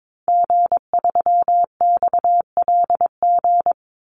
Morse code, heard as a keyed tone, about 22 wpm, 700 Hz. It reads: G3XLG
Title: Call Signs - 22wpm